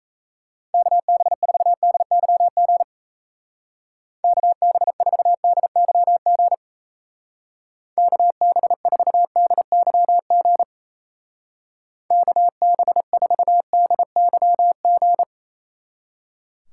Morse code, heard as a keyed tone, about 34 wpm, 700 Hz. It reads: KB4DYG KB4DYG KB4DYG KB4DYG
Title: Speed Racing - USA 2x3 Call Signs 28wpm